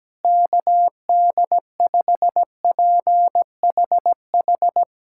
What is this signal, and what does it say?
Morse code, heard as a keyed tone, about 17 wpm, 700 Hz. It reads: KD5PHH
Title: USA Callsign DE Callsign 17wpm